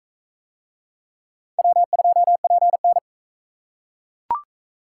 Morse code, heard as a keyed tone, about 38 wpm, 700 Hz. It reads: W2PN